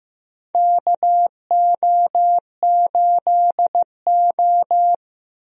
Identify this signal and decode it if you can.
Morse code, heard as a keyed tone, about 15 wpm, 700 Hz. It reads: KO8O
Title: USA Call Signs - 4 Characters - Repeated 2x - 15wpm